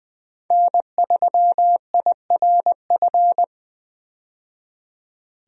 Morse code, heard as a keyed tone, about 20 wpm, 700 Hz. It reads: N3IRF
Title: USA Call Signs - 5 Characters - 20wpm